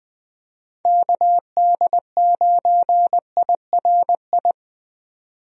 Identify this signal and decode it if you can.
Morse code, heard as a keyed tone, about 20 wpm, 700 Hz. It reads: KD9IRI
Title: USA Call Signs - 6 Characters - 20wpm